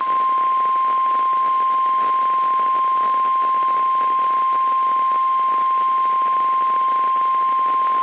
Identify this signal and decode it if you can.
Emergency Alert System (SAME) tones (the 1050 Hz attention tone). It is heard from 0.8 to 7.5 s.
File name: eas / KSDZKDJL / KSDZKDJL